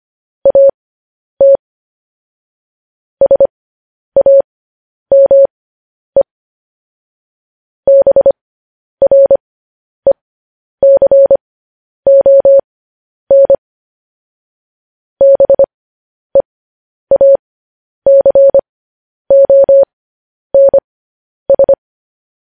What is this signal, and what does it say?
Morse code, heard as a keyed tone, about 25 wpm, spaced as for 10 wpm, 550 Hz. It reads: AT SAME BRECON BEACONS